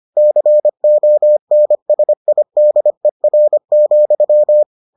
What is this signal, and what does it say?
Morse code, heard as a keyed tone, about 25 wpm, 600 Hz. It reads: CONSIDER,